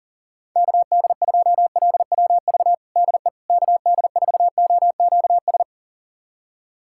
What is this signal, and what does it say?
Morse code, heard as a keyed tone, about 40 wpm, 700 Hz. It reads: KD2LWV DE KD4OQS
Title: USA Callsign DE Callsign - 6 Characters - 40wpm